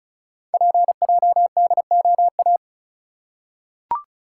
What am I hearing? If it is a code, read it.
Morse code, heard as a keyed tone, about 36 wpm, 700 Hz. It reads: PJDOA
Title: Code Groups - 5 Letters - 35wpm